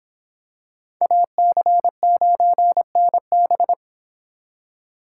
Morse code, heard as a keyed tone, about 26 wpm, 700 Hz. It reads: AC9NB